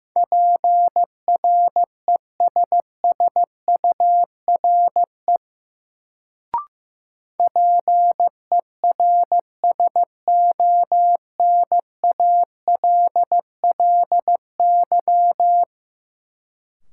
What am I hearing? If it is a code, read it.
Morse code, heard as a keyed tone, about 15 wpm, 700 Hz. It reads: PRESSURE PERSONALLY
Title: Sentences from Top 1400 Words - New Word Drill Alt - 15wpm